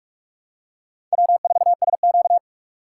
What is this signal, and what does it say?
Morse code, heard as a keyed tone, about 43 wpm, 700 Hz. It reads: W4SQ
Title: USA Call Signs - 4x Character Spacing - 45wpm